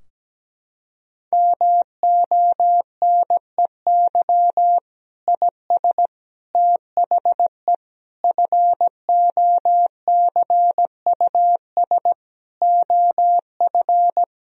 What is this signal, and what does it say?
Morse code, heard as a keyed tone, about 17 wpm, 700 Hz. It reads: MONEY IS THE FOCUS OF
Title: Sentences from Top 1000 Words - Review Part 4 - 17wpm